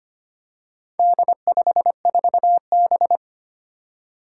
Morse code, heard as a keyed tone, about 25 wpm, 700 Hz. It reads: D54B
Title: Four Challenging Letter-Number Combinations 25wpm